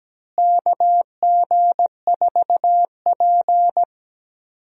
Morse code, heard as a keyed tone, about 17 wpm, 700 Hz. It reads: KG4P
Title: USA Call Signs - 4 Characters - Repeated 3x - 17wpm